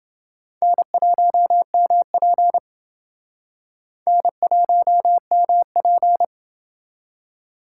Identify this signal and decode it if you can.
Morse code, heard as a keyed tone, about 29 wpm, 700 Hz. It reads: N1MP N1MP